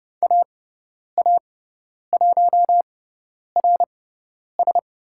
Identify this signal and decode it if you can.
Morse code, heard as a keyed tone, about 30 wpm, spaced as for 10 wpm, 700 Hz. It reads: AA1RS
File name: Lesson 18g - Letter m - Call Signs 30wpm